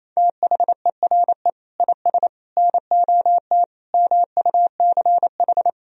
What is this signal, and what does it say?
Morse code, heard as a keyed tone, about 28 wpm, 700 Hz. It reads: THERE IS NOT MUCH
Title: Sentences from Top 1000 Words - Review Part 4 - 28wpm